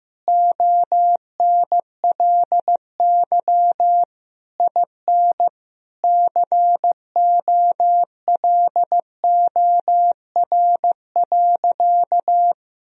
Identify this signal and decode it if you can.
Morse code, heard as a keyed tone, about 15 wpm, 700 Hz. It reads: ONLY IN COLOR.
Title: Sentences from Top 1000 Words - Review Part 2 - 15wpm